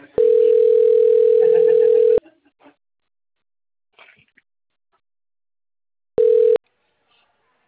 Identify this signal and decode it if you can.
Ringback tone (North America), 440 + 480 Hz.